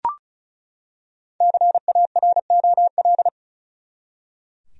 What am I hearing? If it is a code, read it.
Morse code, heard as a keyed tone, about 34 wpm, 700 Hz. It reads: CAROL
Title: USA Common Names - Max 5 Letters - 35wpm